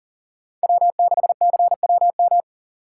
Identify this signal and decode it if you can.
Morse code, heard as a keyed tone, about 38 wpm, 700 Hz. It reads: W6CWM
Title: K1USN SST QSO Practice - 40wpm